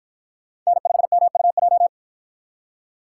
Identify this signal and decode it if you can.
Morse code, heard as a keyed tone, about 53 wpm, 700 Hz. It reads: N5MUJ